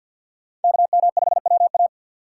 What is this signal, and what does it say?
Morse code, heard as a keyed tone, about 53 wpm, 700 Hz. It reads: KM5WA
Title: USA Call Signs - 5 Characters - 3x Character Spacing - 50wpm